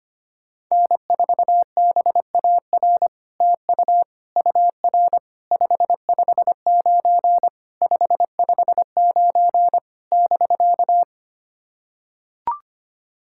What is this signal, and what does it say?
Morse code, heard as a keyed tone, about 25 wpm, 700 Hz. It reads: N4BAR TU UR 559 559 <BK>
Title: SOTA QSO Practice 25wpm